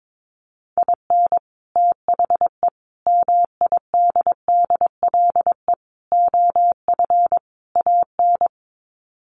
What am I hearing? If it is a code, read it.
Morse code, heard as a keyed tone, about 22 wpm, 700 Hz. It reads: IN THE MIDDLE OF AN
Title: Sets of 5 Words Encore2 22wpm